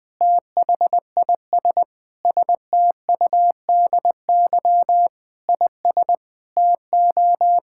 Morse code, heard as a keyed tone, about 19 wpm, 700 Hz. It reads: THIS STUDY IS TO
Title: Sentences with 5-Word Phrases - 20wpm